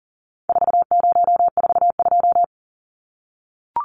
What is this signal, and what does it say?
Morse code, heard as a keyed tone, about 38 wpm, 700 Hz. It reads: 4042
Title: Numbers - 4-Digit 40wpm